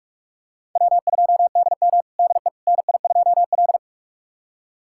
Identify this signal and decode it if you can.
Morse code, heard as a keyed tone, about 44 wpm, 690 Hz. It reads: W2DM DE NI2L
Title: USA Callsign DE Callsign - 4 Characters - 45wpm